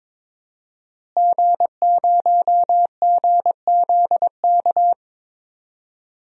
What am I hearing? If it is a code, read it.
Morse code, heard as a keyed tone, about 22 wpm, 700 Hz. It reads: G0GZK
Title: DXCC-100 Call Signs - 5 Characters - Repeated 3x - 22wpm